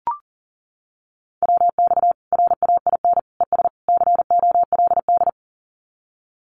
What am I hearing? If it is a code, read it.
Morse code, heard as a keyed tone, about 40 wpm, 700 Hz. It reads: WX RAIN ES COLD
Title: Common QSO Phrases 40wpm